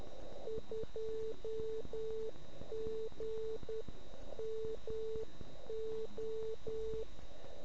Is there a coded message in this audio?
Yes: Morse code from 0.5 to 7.0 s, heard as a keyed tone, about 10 wpm, 440 Hz. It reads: 2GMO